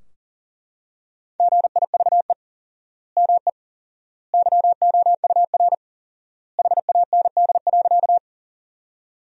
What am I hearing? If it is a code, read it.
Morse code, heard as a keyed tone, about 40 wpm, 700 Hz. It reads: GIVE ME YOUR HAND.
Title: 4x Word-Spacing - Sentences from Top 200 Words - 40wpm